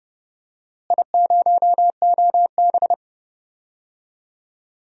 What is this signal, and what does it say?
Morse code, heard as a keyed tone, about 30 wpm, 700 Hz. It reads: I0OB
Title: DXCC-100 Call Signs - 4 Characters - Repeated 3x - 30wpm